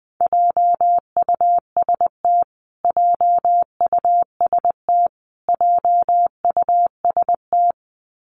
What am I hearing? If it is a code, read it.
Morse code, heard as a keyed tone, about 19 wpm, 700 Hz. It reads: JUST JUST JUST
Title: Top 500 Words Repeated 20wpm